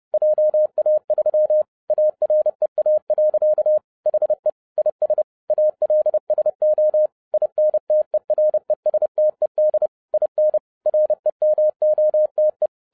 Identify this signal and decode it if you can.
Morse code, heard as a keyed tone, about 30 wpm, 610 Hz. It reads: JA3 AREA. HE IS ALSO INTERESTED IN REMOTE